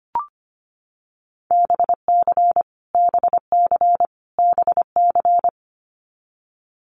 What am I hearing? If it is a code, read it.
Morse code, heard as a keyed tone, about 25 wpm, 700 Hz. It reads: BC BC BC
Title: Canadian Provinces and Territories - Repeated 3x - 25wpm